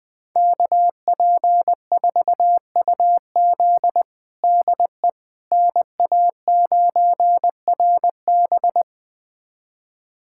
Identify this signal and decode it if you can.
Morse code, heard as a keyed tone, about 20 wpm, 700 Hz. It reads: KP4UZ DE NA9RB
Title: USA Callsign DE Callsign - 5 Characters - 20wpm